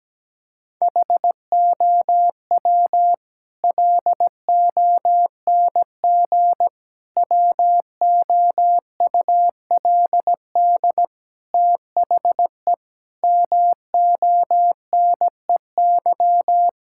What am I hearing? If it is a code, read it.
Morse code, heard as a keyed tone, about 17 wpm, 700 Hz. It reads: HOW LONG WOULD THE MONEY
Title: Sentences from Top 1000 Words - Review Part 5 - 17wpm